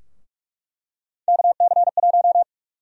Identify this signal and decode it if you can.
Morse code, heard as a keyed tone, about 46 wpm, 700 Hz. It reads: KC1
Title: USA Call Sign Prefixes - 45wpm